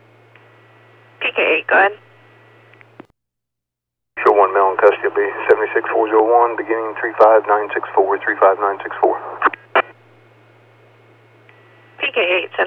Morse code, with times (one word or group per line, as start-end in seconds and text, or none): none